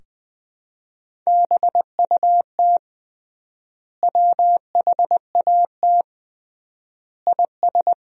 Morse code, heard as a keyed tone, about 20 wpm, 700 Hz. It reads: BUT WHAT IS
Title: 3x Word-Spacing - Sentences from Top 100 Words - 20wpm